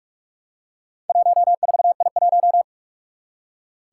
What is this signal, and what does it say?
Morse code, heard as a keyed tone, about 44 wpm, 700 Hz. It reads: 14I1
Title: Four Challenging Letter-Number Combinations 45wpm